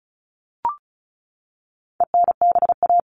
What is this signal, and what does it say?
Morse code, heard as a keyed tone, about 34 wpm, 700 Hz. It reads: ENBA
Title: All Cut Numbers - 4-Digit 35wpm